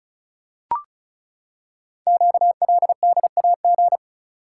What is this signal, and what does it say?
Morse code, heard as a keyed tone, about 36 wpm, 700 Hz. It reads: QLDAG
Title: Code Groups - 5 Letters - 35wpm